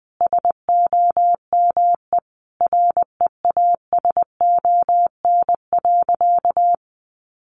Morse code, heard as a keyed tone, about 20 wpm, 700 Hz. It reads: SOME REASON.
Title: Sentences from Top 700 Words 20wpm